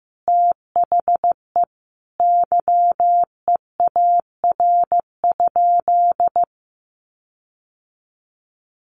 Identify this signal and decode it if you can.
Morse code, heard as a keyed tone, about 15 wpm, 700 Hz. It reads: THE YEAR?
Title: Sentences from Top 1400 Words 15wpm